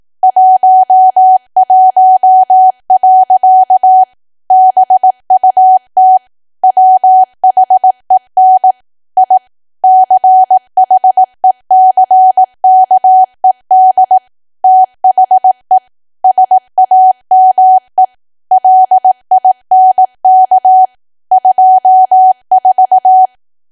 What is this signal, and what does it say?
Morse code, heard as a keyed tone, about 18 wpm, 750 Hz. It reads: 11. BUT WHEN I CHECKED THE SAME LINK 24
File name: cwa / W1AW-Level2 / 18WPM